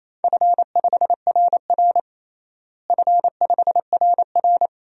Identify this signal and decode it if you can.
Morse code, heard as a keyed tone, about 28 wpm, 700 Hz. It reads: F5RR F5RR
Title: DXCC-100 Call Signs - 4 Characters - Repeated 2x - 28wpm